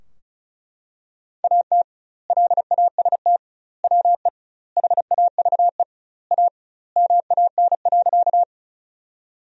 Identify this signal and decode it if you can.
Morse code, heard as a keyed tone, about 36 wpm, 700 Hz. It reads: AT LAST WE HAVE A MAN.
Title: 2x Word-Spacing - Sentences from Top 300 Words - 35wpm